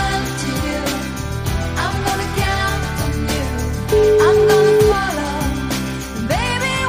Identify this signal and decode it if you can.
Ringback tone (Europe, 425 Hz), with noise.